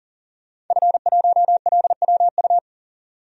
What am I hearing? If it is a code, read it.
Morse code, heard as a keyed tone, about 40 wpm, 700 Hz. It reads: F1LWU
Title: DXCC-100 Call Signs - 5 Characters - 40wpm